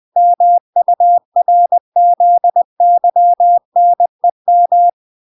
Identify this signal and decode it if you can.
Morse code, heard as a keyed tone, about 20 wpm, 700 Hz. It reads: MURZYNEM